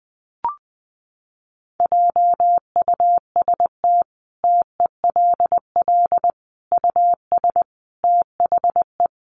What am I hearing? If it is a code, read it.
Morse code, heard as a keyed tone, about 20 wpm, 700 Hz. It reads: JUST TELL US THE
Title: Sentences from Top 800 Words 20wpm